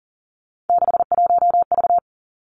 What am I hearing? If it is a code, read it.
Morse code, heard as a keyed tone, about 40 wpm, 700 Hz. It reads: 61V
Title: Three Challenging Letter-Number Combinations 40wpm